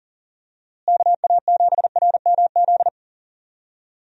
Morse code, heard as a keyed tone, about 40 wpm, 700 Hz. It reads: KA7RMZ
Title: USA Call Signs - 6 Characters - Repeated 2x - 40wpm